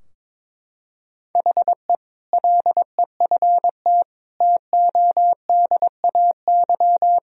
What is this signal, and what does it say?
Morse code, heard as a keyed tone, about 22 wpm, 700 Hz. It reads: HE LEFT TODAY
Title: Sentences from Top 1000 Words - Review Part 2 - 22wpm